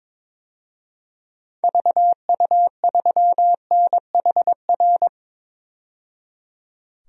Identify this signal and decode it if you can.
Morse code, heard as a keyed tone, about 22 wpm, 700 Hz. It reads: VU3NHR
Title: DXCC-100 Call Signs - 6 Characters - 22wpm